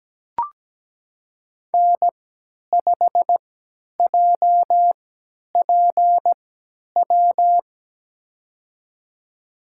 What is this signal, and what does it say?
Morse code, heard as a keyed tone, about 17 wpm, spaced as for 10 wpm, 700 Hz. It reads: N5JPW
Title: USA Call Signs - 5 Characters - 3x Character Spacing - 17wpm